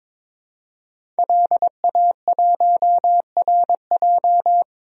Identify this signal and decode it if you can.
Morse code, heard as a keyed tone, about 22 wpm, 700 Hz. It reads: LA1RJ
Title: DXCC-100 Call Signs - 5 Characters - 22wpm